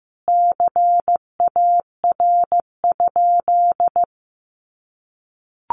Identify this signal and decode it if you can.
Morse code, heard as a keyed tone, about 15 wpm, 700 Hz. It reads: CAR?
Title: Sentences from Top 1500 Words 15wpm